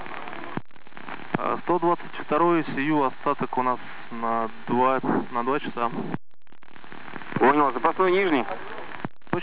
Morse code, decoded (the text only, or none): none